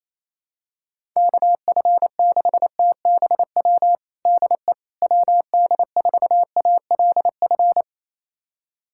Morse code, heard as a keyed tone, about 28 wpm, 700 Hz. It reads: KF6TBW DE WD4ALF
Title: USA Callsign DE Callsign - 6 Characters - 28wpm